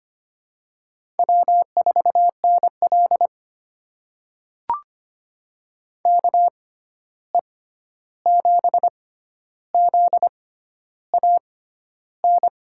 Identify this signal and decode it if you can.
Morse code, heard as a keyed tone, about 25 wpm, spaced as for 11 wpm, 700 Hz. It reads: W4NL KE7ZAN
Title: USA Call Signs - 6x Character Spacing - 25wpm